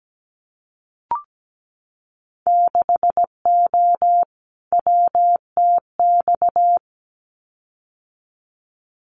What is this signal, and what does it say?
Morse code, heard as a keyed tone, about 17 wpm, 700 Hz. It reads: 6O WTX
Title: Winter Field Day Exchanges Encore 17wpm